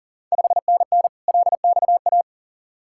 Morse code, heard as a keyed tone, about 38 wpm, 690 Hz. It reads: 5NN LXA
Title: Ontario QSO Party Exchange 40wpm